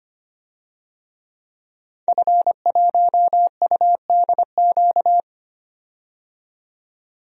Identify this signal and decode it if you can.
Morse code, heard as a keyed tone, about 25 wpm, 700 Hz. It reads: F1UDQ
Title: DXCC-100 Call Signs - 5 Characters - 25wpm